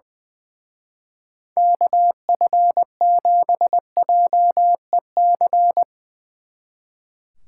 Morse code, heard as a keyed tone, about 20 wpm, 700 Hz. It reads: KF7JEC
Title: Speed Racing - USA 2x3 Call Signs 20wpm